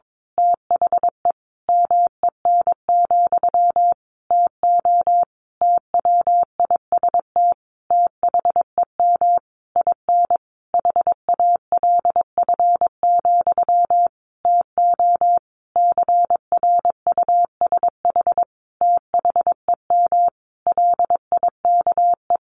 Morse code, heard as a keyed tone, about 22 wpm, 700 Hz. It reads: THE MEN, TO TWIST THEM IN HALF, TO CRUSH THEM LIKE
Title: A Sound Of Thunder 22wpm